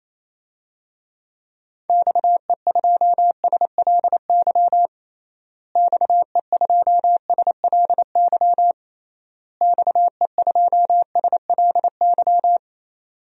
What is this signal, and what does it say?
Morse code, heard as a keyed tone, about 28 wpm, 700 Hz. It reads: XE2SLY XE2SLY XE2SLY
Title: DXCC-100 Call Signs - 6 Characters - Repeated 3x - 28wpm